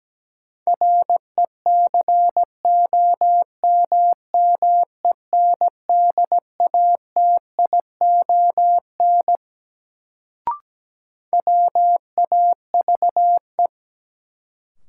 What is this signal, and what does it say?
Morse code, heard as a keyed tone, about 17 wpm, 700 Hz. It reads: RECOMMENDATION WAVE
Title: Sentences from Top 1400 Words - New Word Drill - 17wpm